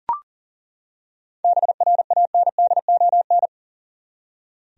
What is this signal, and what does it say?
Morse code, heard as a keyed tone, about 40 wpm, 700 Hz. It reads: BRANDON
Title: USA Common Names - 40wpm